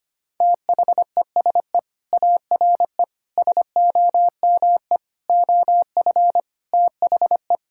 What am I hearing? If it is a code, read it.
Morse code, heard as a keyed tone, about 25 wpm, 700 Hz. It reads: THESE ARE SOME OF THE
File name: practice / 25wpm / Sentences with 4-Word Phrases - 25wpm